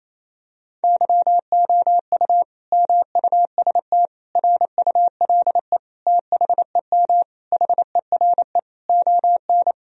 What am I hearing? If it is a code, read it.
Morse code, heard as a keyed tone, about 28 wpm, 700 Hz. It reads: YOU MUST RULE THEM HERE ON
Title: Sentences from Top 1000 Words - Review Part 2 - 28wpm